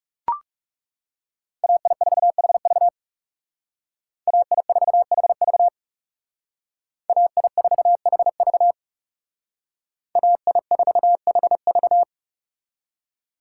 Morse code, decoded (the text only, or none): AI4HV AI4HV AI4HV AI4HV